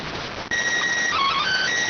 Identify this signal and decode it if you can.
SSTV (slow-scan television), PD 290.